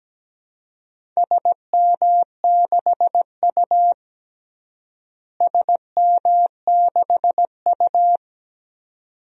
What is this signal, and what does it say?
Morse code, heard as a keyed tone, about 17 wpm, 700 Hz. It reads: SM6U SM6U
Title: DXCC-100 Call Signs - 4 Characters - Repeated 2x - 17wpm